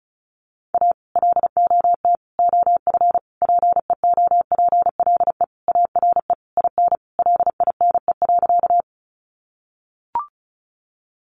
Morse code, heard as a keyed tone, about 34 wpm, 700 Hz. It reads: A LOT OF PEOPLE ARE IN LINE.
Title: Additional Easy Phrases 35wpm